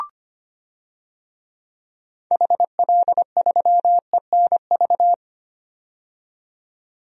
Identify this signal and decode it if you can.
Morse code, heard as a keyed tone, about 25 wpm, 700 Hz. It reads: HL3ENV
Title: DXCC-100 Call Signs - 6 Characters - 25wpm